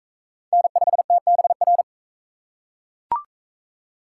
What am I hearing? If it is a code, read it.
Morse code, heard as a keyed tone, about 38 wpm, 700 Hz. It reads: N5TBR